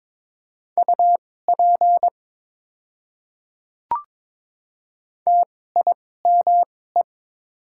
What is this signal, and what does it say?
Morse code, heard as a keyed tone, about 22 wpm, spaced as for 16 wpm, 700 Hz. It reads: UP TIME
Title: 2x Character Spacing - Top 200 Words - 22wpm